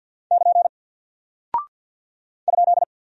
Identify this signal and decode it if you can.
Morse code, heard as a keyed tone, about 48 wpm, 700 Hz. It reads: <KN> ?
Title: Single Character (Letter:Number:Punctuation:Prosign) 50wpm